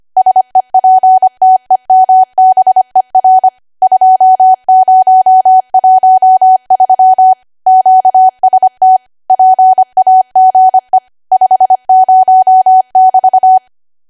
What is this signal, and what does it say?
Morse code, heard as a keyed tone, about 25 wpm, 750 Hz. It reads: SEPTEMBER 2013 QST PAGE 50<BT>